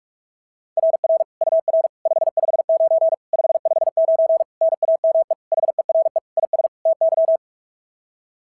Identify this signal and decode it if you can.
Morse code, heard as a keyed tone, about 46 wpm, 690 Hz. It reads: RR UR 559 559 NAME HERE IS TY
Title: SOTA QSO Practice 45wpm